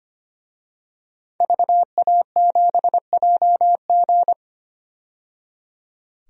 Morse code, heard as a keyed tone, about 25 wpm, 700 Hz. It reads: VA7JG
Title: DXCC-100 Call Signs - 5 Characters - 25wpm